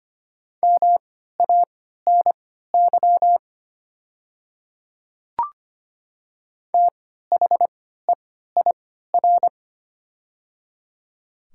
Morse code, heard as a keyed tone, about 25 wpm, spaced as for 14 wpm, 700 Hz. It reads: MANY THEIR
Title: 3x Character Spacing - Top 100 Words - 25wpm